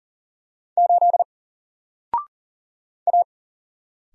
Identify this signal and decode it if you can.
Morse code, heard as a keyed tone, about 39 wpm, 700 Hz. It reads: 8 A
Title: Single Letter-Number 40wpm